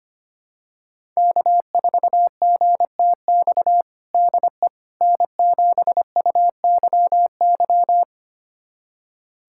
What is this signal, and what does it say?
Morse code, heard as a keyed tone, about 25 wpm, 700 Hz. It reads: K4GTX DE N7UYY
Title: USA Callsign DE Callsign 25wpm